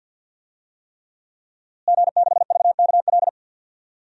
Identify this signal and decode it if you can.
Morse code, heard as a keyed tone, about 48 wpm, 700 Hz. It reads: G6VKL
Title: DXCC-100 Call Signs - 5 Characters - Repeated 3x - 50wpm